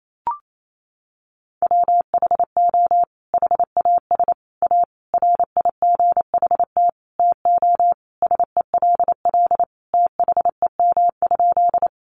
Morse code, heard as a keyed tone, about 28 wpm, 700 Hz. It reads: WHO HAS A RIGHT TO SELL THEM?
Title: Sentences from Top 1100 Words 28wpm